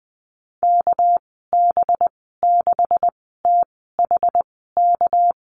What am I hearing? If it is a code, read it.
Morse code, heard as a keyed tone, about 20 wpm, spaced as for 15 wpm, 700 Hz. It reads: KB6THK
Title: 238 Lesson 38g - Letter q - Call Signs 20wpm